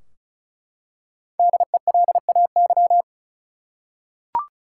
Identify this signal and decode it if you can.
Morse code, heard as a keyed tone, about 36 wpm, 700 Hz. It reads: DELAY
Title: Five Letter Words - 35wpm